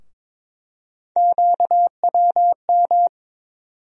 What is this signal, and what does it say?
Morse code, heard as a keyed tone, about 21 wpm, 700 Hz. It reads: QWM
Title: USA Call Sign Postfixes - 3 Letters 22wpm